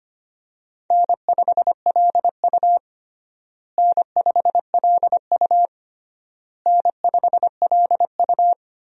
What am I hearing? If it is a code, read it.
Morse code, heard as a keyed tone, about 25 wpm, 700 Hz. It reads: N5LU N5LU N5LU
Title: USA Call Signs - 4 Characters - Repeated 3x - 25wpm